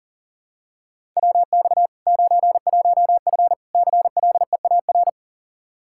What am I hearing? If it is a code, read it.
Morse code, heard as a keyed tone, about 40 wpm, 700 Hz. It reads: WX 91F CLEAR
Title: K1USN SST QSO Practice 40wpm